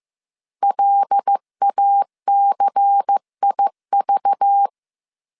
Morse code, heard as a keyed tone, about 15 wpm, 800 Hz. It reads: LACIV
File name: CW / DET / 1 6 1500C S C 14wpm 2951